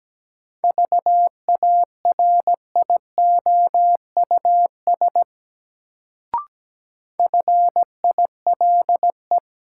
Morse code, heard as a keyed tone, about 17 wpm, 700 Hz. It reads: VARIOUS FILE
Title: Sentences from Top 1000 Words - Review Part 4 - New Word Drill - 17wpm